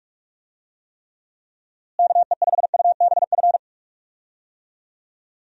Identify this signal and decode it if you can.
Morse code, heard as a keyed tone, about 46 wpm, 700 Hz. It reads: KE5UBF
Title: USA Call Signs - 6 Characters - Repeated 2x - 45wpm